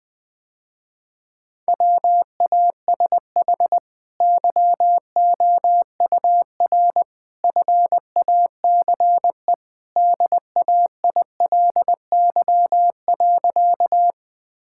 Morse code, heard as a keyed tone, about 20 wpm, 700 Hz. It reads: WASH YOUR FACE DAILY.
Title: Easy Sentences with Hints - 20wpm